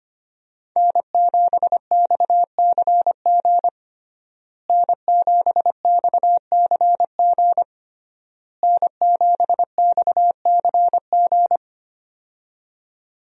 Morse code, heard as a keyed tone, about 25 wpm, 700 Hz. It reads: N7XCG N7XCG N7XCG
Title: USA Call Signs - 5 Characters - Repeated 3x - 25wpm